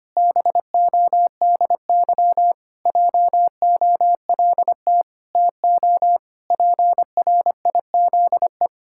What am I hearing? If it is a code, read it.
Morse code, heard as a keyed tone, about 25 wpm, 700 Hz. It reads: BODY JOLT TO PRIZE
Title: Pangram Challenge v2 - 25wpm